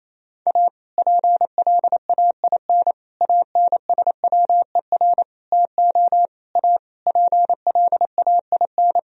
Morse code, heard as a keyed tone, about 28 wpm, 700 Hz. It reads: A PLAIN ANSWER TO A PLAIN
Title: Sentences from Top 1000 Words - Review Part 2 - 28wpm